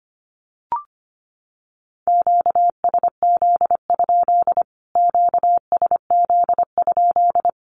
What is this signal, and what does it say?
Morse code, heard as a keyed tone, about 25 wpm, 700 Hz. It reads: QSZ? QSZ?
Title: Q-Codes Repeated 3x 25wpm